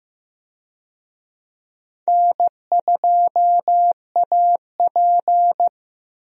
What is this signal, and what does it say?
Morse code, heard as a keyed tone, about 15 wpm, 700 Hz. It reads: N2AP
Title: USA Call Signs - 4 Characters - Repeated 3x - 15wpm